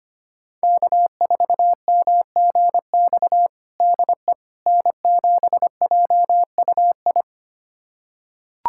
Morse code, heard as a keyed tone, about 25 wpm, 700 Hz. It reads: K4MGX DE N7JUI
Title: USA Callsign DE Callsign - 5 Characters - 25wpm